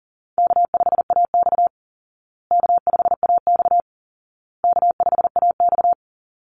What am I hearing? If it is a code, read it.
Morse code, heard as a keyed tone, about 40 wpm, 700 Hz. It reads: K5AX K5AX K5AX
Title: Easy Call Signs Repeated 3x - 40wpm